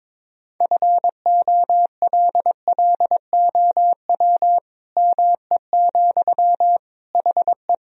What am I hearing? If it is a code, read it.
Morse code, heard as a keyed tone, about 22 wpm, 700 Hz. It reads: FOLLOW ME, HE
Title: Sentences from Top 1000 Words - Review Part 1 - 22wpm